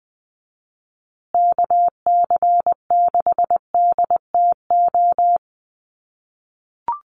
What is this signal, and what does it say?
Morse code, heard as a keyed tone, about 20 wpm, 700 Hz. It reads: KC6DTO
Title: USA Call Signs Repeated 3x 20wpm